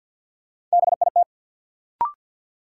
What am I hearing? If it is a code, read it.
Morse code, heard as a keyed tone, about 50 wpm, 700 Hz. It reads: BIT
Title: Sentences from Top 900 Words - New Word Drill - 50wpm